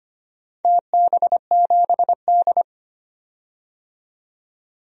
Morse code, heard as a keyed tone, about 25 wpm, 700 Hz. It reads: TB7D
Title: DXCC-100 Call Signs - 4 Characters - 25wpm